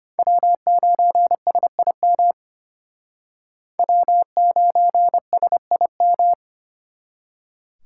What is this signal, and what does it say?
Morse code, heard as a keyed tone, about 32 wpm, 700 Hz. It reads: W9SIM W9SIM